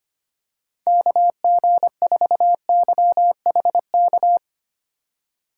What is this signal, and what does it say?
Morse code, heard as a keyed tone, about 25 wpm, 700 Hz. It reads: KG4YHK
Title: USA Call Signs - 6 Characters - 3x Character Spacing - 25wpm